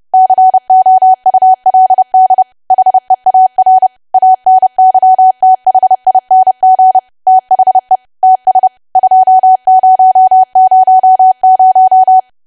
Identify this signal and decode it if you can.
Morse code, heard as a keyed tone, about 30 wpm, 750 Hz. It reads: COULD HEAR ANYTHING THE TS 2000
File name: cwa / W1AW-Level3 / 30WPM